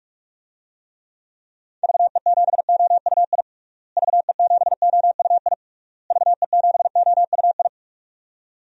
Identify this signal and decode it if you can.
Morse code, heard as a keyed tone, about 41 wpm, 700 Hz. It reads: VE7OUI VE7OUI VE7OUI
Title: DXCC-100 Call Signs - 6 Characters - Repeated 3x - 45wpm